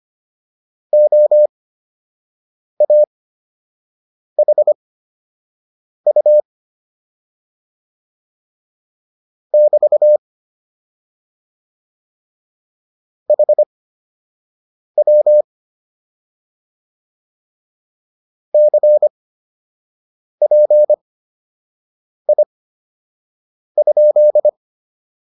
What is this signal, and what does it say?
Morse code, heard as a keyed tone, about 25 wpm, spaced as for 6 wpm, 600 Hz. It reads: OAHU <BT> HW CPI?